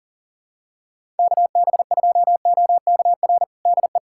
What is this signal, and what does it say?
Morse code, heard as a keyed tone, about 40 wpm, 700 Hz. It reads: KB2OKR DE
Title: USA Callsign DE Callsign - 6 Characters - 40wpm